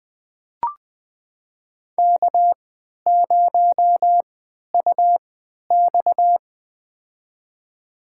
Morse code, heard as a keyed tone, about 21 wpm, spaced as for 12 wpm, 700 Hz. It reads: K0UX
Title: USA Call Signs - 4 Characters - 3x Character Spacing - 20wpm